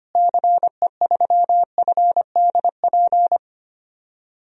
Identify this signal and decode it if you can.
Morse code, heard as a keyed tone, about 25 wpm, 700 Hz. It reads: CE3FDP
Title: DXCC-100 Call Signs - 6 Characters - Repeated 2x - 25wpm